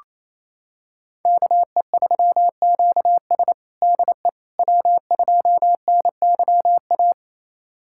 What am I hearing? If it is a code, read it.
Morse code, heard as a keyed tone, about 28 wpm, 700 Hz. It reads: KE3QS DE W2NYA
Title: USA Callsign DE Callsign - 5 Characters - 28wpm